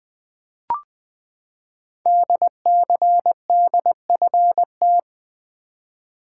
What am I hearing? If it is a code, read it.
Morse code, heard as a keyed tone, about 20 wpm, 700 Hz. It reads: DCDFT
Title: Code Groups - 5 Letters - 20wpm